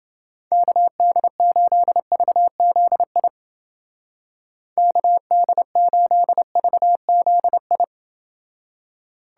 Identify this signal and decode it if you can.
Morse code, heard as a keyed tone, about 30 wpm, 700 Hz. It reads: KD8VZI KD8VZI